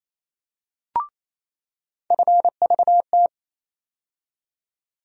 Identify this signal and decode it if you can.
Morse code, heard as a keyed tone, about 28 wpm, 700 Hz. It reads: FVT
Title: USA Call Sign Postfixes - 3 Letters 28wpm